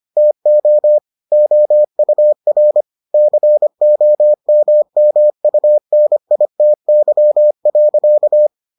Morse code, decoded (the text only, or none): TO OUR COMMUNITY.